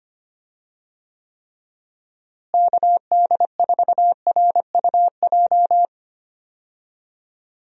Morse code, heard as a keyed tone, about 25 wpm, 700 Hz. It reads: KD4RUJ
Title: USA Call Signs - 6 Characters - Repeated 2x - 25wpm